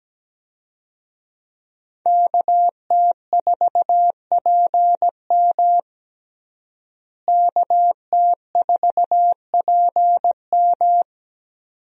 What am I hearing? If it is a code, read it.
Morse code, heard as a keyed tone, about 17 wpm, 700 Hz. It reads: KT4PM KT4PM
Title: USA Call Signs - 5 Characters - Repeated 2x - 17wpm